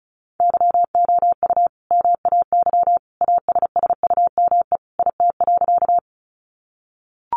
Morse code, heard as a keyed tone, about 34 wpm, 700 Hz. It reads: YOU MAY ASSUME IT.
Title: Sentences from Top 1500 Words 35wpm